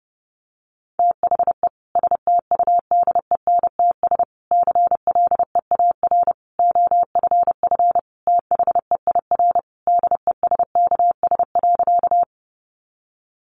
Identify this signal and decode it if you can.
Morse code, heard as a keyed tone, about 30 wpm, 700 Hz. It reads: THE STUDENTS CLEAR OFF THEIR DESKS.
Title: Additional Easy Phrases 30wpm